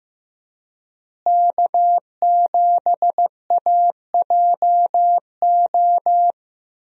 Morse code, heard as a keyed tone, about 16 wpm, 700 Hz. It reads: K7AJO
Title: USA Call Signs - 5 Characters - Repeated 3x - 15wpm